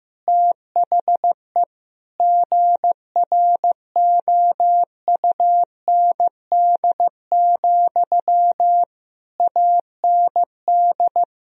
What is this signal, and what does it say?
Morse code, heard as a keyed tone, about 15 wpm, 700 Hz. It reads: THE GROUND, AND
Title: Sentences from Top 1000 Words - Review Part 2 - 15wpm